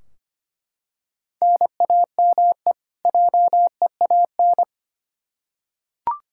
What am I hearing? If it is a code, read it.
Morse code, heard as a keyed tone, about 25 wpm, 700 Hz. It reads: NAME JEAN
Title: Practice QSO - Exchange 1 - Beginning - 25wpm